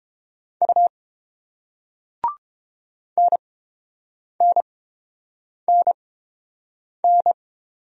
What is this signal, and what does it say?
Morse code, heard as a keyed tone, about 30 wpm, spaced as for 7 wpm, 700 Hz. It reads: U NNNN